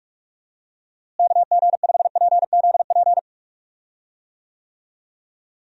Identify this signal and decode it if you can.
Morse code, heard as a keyed tone, about 46 wpm, 700 Hz. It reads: KG5PZP
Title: USA Call Signs - 6 Characters - Repeated 3x - 45wpm